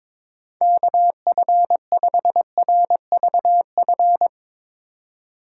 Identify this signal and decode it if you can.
Morse code, heard as a keyed tone, about 21 wpm, 700 Hz. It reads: KF5RVF
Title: USA Call Signs - 3x Character Spacing - 22wpm